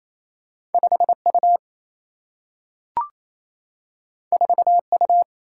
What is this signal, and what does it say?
Morse code, heard as a keyed tone, about 28 wpm, 700 Hz. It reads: 5 U 4 U
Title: Single-Letter Single-Number 28wpm